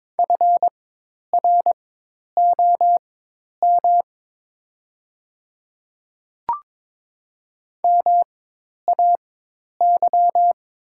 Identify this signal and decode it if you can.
Morse code, heard as a keyed tone, about 22 wpm, spaced as for 10 wpm, 700 Hz. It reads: FROM MAY
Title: 4x Character Spacing - Top 200 Words - 22wpm